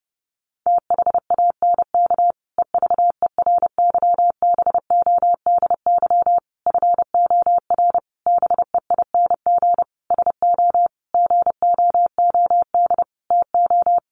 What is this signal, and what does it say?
Morse code, heard as a keyed tone, about 28 wpm, 700 Hz. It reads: THANK EVERYBODY FOR BEING SO GOOD TO
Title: Sentences from Top 1400 Words 30wpm